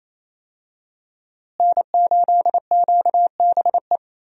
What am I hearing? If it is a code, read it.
Morse code, heard as a keyed tone, about 28 wpm, 700 Hz. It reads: N8QBE
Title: USA Call Signs - 5 Characters - Repeated 2x - 28wpm